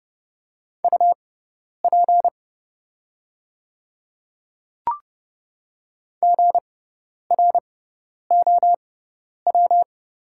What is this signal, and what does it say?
Morse code, heard as a keyed tone, about 30 wpm, spaced as for 10 wpm, 700 Hz. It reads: UP GROW
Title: 6x Character Spacing - Top 500 Words - 30wpm